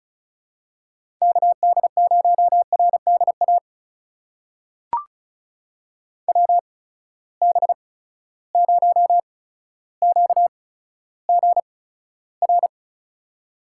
Morse code, heard as a keyed tone, about 36 wpm, spaced as for 15 wpm, 700 Hz. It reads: KD0RDA WB0QGR
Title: USA Call Signs - 6 Characters - 8x Character Spacing - 35wpm